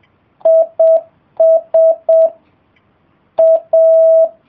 Morse code, heard as a keyed tone, about 7 wpm, 650 Hz. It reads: IS A